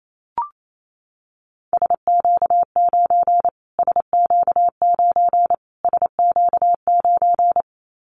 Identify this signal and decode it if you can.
Morse code, heard as a keyed tone, about 28 wpm, 700 Hz. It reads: SQ9 SQ9 SQ9
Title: Callsign Prefixes for Top 25 DXCC - Repeated 3x - 28wpm